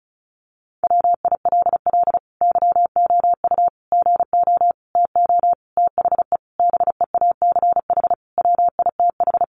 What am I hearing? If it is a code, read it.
Morse code, heard as a keyed tone, about 34 wpm, 700 Hz. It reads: WILL YOU GO TO THE BEACH WITH
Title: Sentences from Top 1500 Words 35wpm